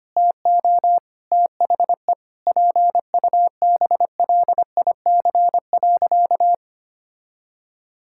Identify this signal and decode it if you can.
Morse code, heard as a keyed tone, about 25 wpm, 700 Hz. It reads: TO THE PUBLIC.
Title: Sentences from Top 1000 Words - Review Part 6 - 25wpm